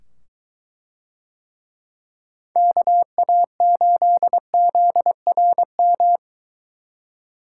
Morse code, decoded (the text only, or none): KA8ZRM